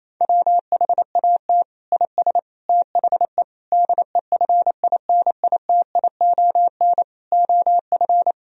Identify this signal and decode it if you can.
Morse code, heard as a keyed tone, about 28 wpm, 700 Hz. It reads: WHAT IS THE DEFINITION OF
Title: Sentences from Top 1000 Words - Review Part 4 - 28wpm